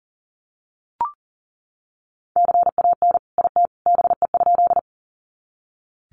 Morse code, heard as a keyed tone, about 40 wpm, 700 Hz. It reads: CAN IT BE?
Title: Sentences from Top 100 Words 40wpm